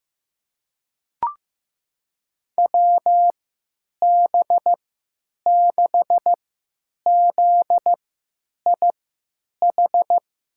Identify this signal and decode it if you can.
Morse code, heard as a keyed tone, about 15 wpm, spaced as for 9 wpm, 700 Hz. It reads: WB6ZIH
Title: USA Call Signs - 6 Characters - 3x Character Spacing - 15wpm